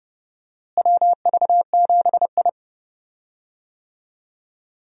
Morse code, heard as a keyed tone, about 30 wpm, 700 Hz. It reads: WV7I
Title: USA Call Signs - 4 Characters - Repeated 3x - 30wpm